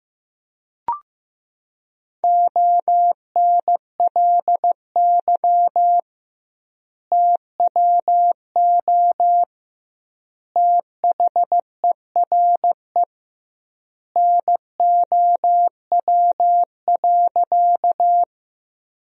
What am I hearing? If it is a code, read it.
Morse code, heard as a keyed tone, about 16 wpm, 700 Hz. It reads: ONLY TWO THERE NOW.
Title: 2x Word-Spacing - Sentences from Top 200 Words - 15wpm